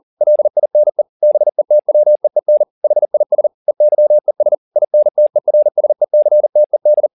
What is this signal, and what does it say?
Morse code, heard as a keyed tone, about 40 wpm, 600 Hz. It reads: LINE BETWEEN HIS EYES INTERSECTED
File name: the old man and the sea 2 40wpm